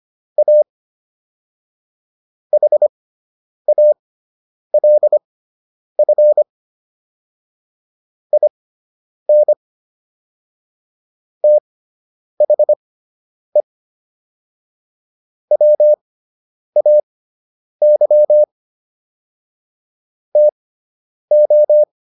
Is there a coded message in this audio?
Yes: Morse code from 0.4 to 21.8 s, heard as a keyed tone, about 25 wpm, spaced as for 9 wpm, 600 Hz. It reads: A HALF IN THE WAY TO